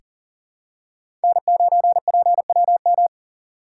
Morse code, heard as a keyed tone, about 40 wpm, 700 Hz. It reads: N9PWM
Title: USA Call Signs - 5 Characters - 2x Character Spacing - 40wpm